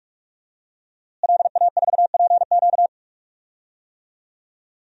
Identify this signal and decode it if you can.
Morse code, heard as a keyed tone, about 46 wpm, 700 Hz. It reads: LA4PQ
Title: DXCC-100 Call Signs - 5 Characters - Repeated 2x - 45wpm